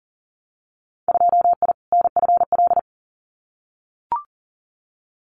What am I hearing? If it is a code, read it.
Morse code, heard as a keyed tone, about 42 wpm, 700 Hz. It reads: 2I NFL
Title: Winter Field Day Exchanges 40wpm